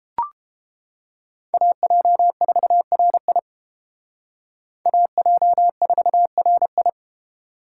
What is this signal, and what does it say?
Morse code, heard as a keyed tone, about 29 wpm, 700 Hz. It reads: AJ4RI AJ4RI